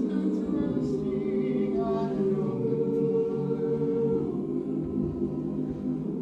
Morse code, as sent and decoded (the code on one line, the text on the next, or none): none